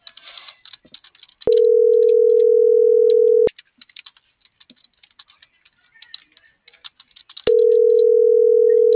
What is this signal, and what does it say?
Ringback tone (North America), 440 + 480 Hz.
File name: CENTRAL MORENA / out ruth.skyangel.24